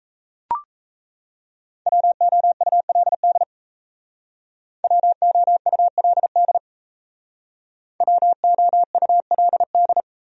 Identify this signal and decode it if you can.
Morse code, heard as a keyed tone, about 35 wpm, 700 Hz. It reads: WOULD WOULD WOULD